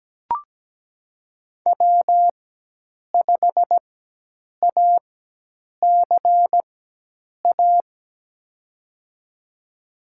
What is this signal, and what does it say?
Morse code, heard as a keyed tone, about 17 wpm, spaced as for 8 wpm, 700 Hz. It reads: W5ACA
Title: USA Call Signs - 5 Characters - 4x Character Spacing - 17wpm